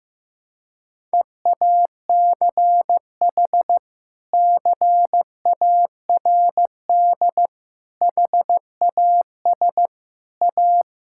Morse code, heard as a keyed tone, about 15 wpm, 700 Hz. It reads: EACH CARD HAS A
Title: Sentences from Top 1000 Words - Review Part 6 - 15wpm